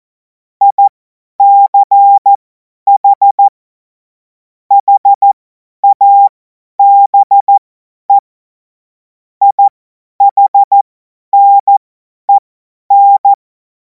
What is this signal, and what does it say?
Morse code, heard as a keyed tone, about 14 wpm, spaced as for 10 wpm, 800 Hz. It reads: ICH HABE IHNEN
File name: Morsekurs / Witze / Witz 19. Mai 2026 50 F70